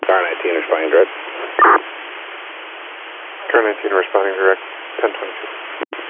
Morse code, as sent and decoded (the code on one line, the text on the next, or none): none